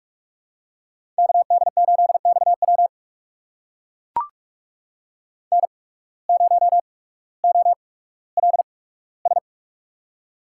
Morse code, heard as a keyed tone, about 44 wpm, spaced as for 19 wpm, 700 Hz. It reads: KD8XW N0OLS
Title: USA Call Signs - 5 Characters - 8x Character Spacing - 45wpm